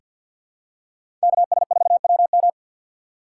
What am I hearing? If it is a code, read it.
Morse code, heard as a keyed tone, about 50 wpm, 700 Hz. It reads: KS4WM
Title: USA Call Signs - 6x Character Spacing - 50wpm